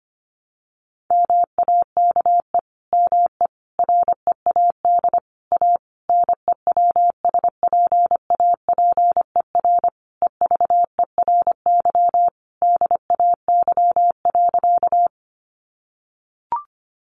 Morse code, heard as a keyed tone, about 25 wpm, 700 Hz. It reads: MAKE ME READ A NEWSPAPER EVERY DAY.
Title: Sentences from Top 1100 Words 25wpm